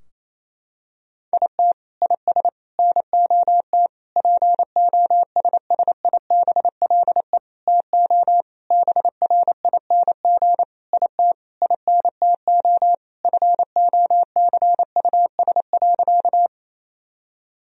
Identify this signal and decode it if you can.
Morse code, heard as a keyed tone, about 28 wpm, 700 Hz. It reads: IT IS NOT POSSIBLE TO BRING IT INTO FOCUS.
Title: Sentences from Top 1000 Words - Review Part 4 - 28wpm